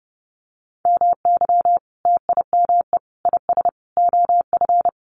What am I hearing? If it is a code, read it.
Morse code, heard as a keyed tone, about 30 wpm, 700 Hz. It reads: MY TIME IS OF
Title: Sentences from Top 600 Words 30wpm